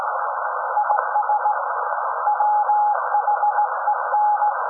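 Morse code, heard as a keyed tone, about 18 wpm, 840 Hz. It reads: IS WHEN
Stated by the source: SAQ Grimeton Radio, Alexanderson 200 kW alternator on 17.2 kHz, received in Potsdam with a very strong signal on World Radio Day.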